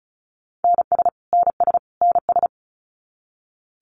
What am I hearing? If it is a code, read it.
Morse code, heard as a keyed tone, about 34 wpm, 700 Hz. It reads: NS NS NS
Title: Canadian Provinces and Territories - Repeated 3x - 35wpm